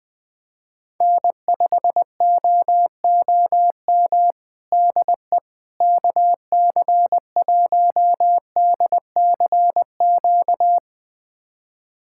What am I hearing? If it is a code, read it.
Morse code, heard as a keyed tone, about 20 wpm, 700 Hz. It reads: N5OOM DE KC1DCQ
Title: USA Callsign DE Callsign 20wpm